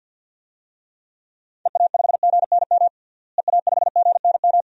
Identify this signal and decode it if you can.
Morse code, heard as a keyed tone, about 49 wpm, 700 Hz. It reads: EA5GNM EA5GNM
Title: DXCC-100 Call Signs - 6 Characters - Repeated 2x - 50wpm